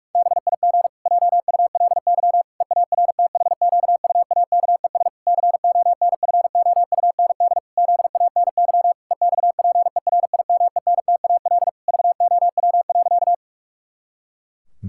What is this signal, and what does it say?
Morse code, heard as a keyed tone, about 46 wpm, 700 Hz. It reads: BIG JULY EARTHQUAKES CONFOUND ZANY EXPERIMENTAL VOW.
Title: Pangram Challenge v2 - 45wpm